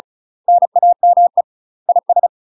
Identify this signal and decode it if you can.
Morse code, heard as a keyed tone, about 35 wpm, 700 Hz. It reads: NAME IS
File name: UK MASTER QSOs TRAINING / 35WPM / UK MASTER QSOs 0013